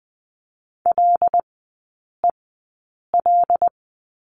Morse code, heard as a keyed tone, about 20 wpm, spaced as for 8 wpm, 700 Hz. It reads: LEL
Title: 074 Lesson 13e - Letter L - Three Characters 20wpm